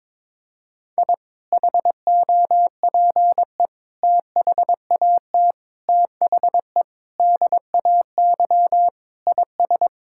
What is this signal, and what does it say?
Morse code, heard as a keyed tone, about 22 wpm, 700 Hz. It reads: I HOPE THAT THE DAY IS
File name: practice / 22wpm / Sentences with 2-Word Phrases - 22wpm